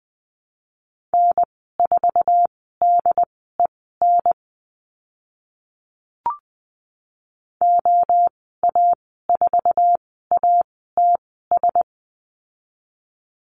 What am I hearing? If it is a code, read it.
Morse code, heard as a keyed tone, about 20 wpm, spaced as for 14 wpm, 700 Hz. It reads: N4DEN OA4ATS
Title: 069 Lesson 12g - Letter d - Call Signs 20wpm